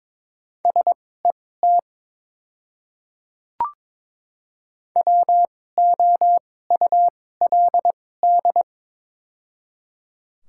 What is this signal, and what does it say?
Morse code, heard as a keyed tone, about 22 wpm, spaced as for 16 wpm, 700 Hz. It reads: SET WOULD
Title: 2x Character Spacing - Top 300 Words - 22wpm